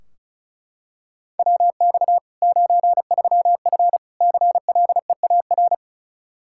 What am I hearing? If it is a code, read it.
Morse code, heard as a keyed tone, about 36 wpm, 700 Hz. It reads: WX 93F CLEAR
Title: Practice QSO - Exchange 2 - Info - 35wpm